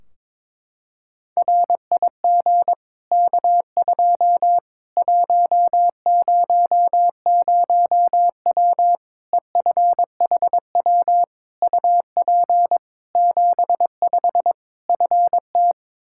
Morse code, heard as a keyed tone, about 22 wpm, 700 Hz. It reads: RIG K2 100W EFHW UP 75 FT
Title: Practice QSO - Exchange 2 - Info - 22wpm